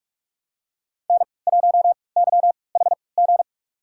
Morse code, heard as a keyed tone, about 46 wpm, spaced as for 25 wpm, 700 Hz. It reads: N1YHG
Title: USA Call Signs - 3x Character Spacing - 45wpm